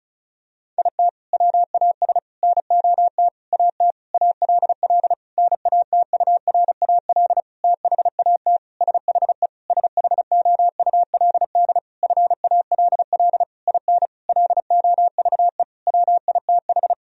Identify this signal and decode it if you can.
Morse code, heard as a keyed tone, about 36 wpm, 700 Hz. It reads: IT WAS NOT AT ALL NATURAL THAT SHE SHOULD FALL IN LOVE WITH
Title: Sentences from Top 1000 Words - Review Part 4 - 35wpm